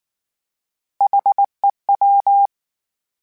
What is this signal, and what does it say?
Morse code, heard as a keyed tone, about 19 wpm, 800 Hz. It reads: HEW
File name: H Prefixes / H mp3s